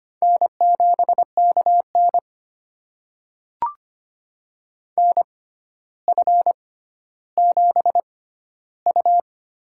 Morse code, heard as a keyed tone, about 25 wpm, spaced as for 13 wpm, 700 Hz. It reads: N7KN NF7U
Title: USA Call Signs - 4 Characters - 6x Character Spacing - 25wpm